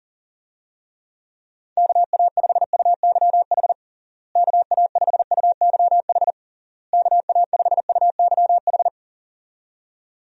Morse code, heard as a keyed tone, about 40 wpm, 700 Hz. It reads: KA5UYH KA5UYH KA5UYH
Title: USA Call Signs - 6 Characters - Repeated 3x - 40wpm